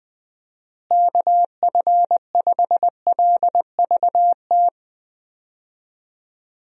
Morse code, heard as a keyed tone, about 20 wpm, 700 Hz. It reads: KF5LVT
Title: USA Call Signs - 6 Characters - Repeated 3x - 20wpm